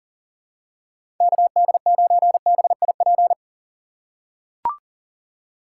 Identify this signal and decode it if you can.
Morse code, heard as a keyed tone, about 40 wpm, 700 Hz. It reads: KD9BIP
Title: USA Call Signs - 6 Characters - 3x Character Spacing - 40wpm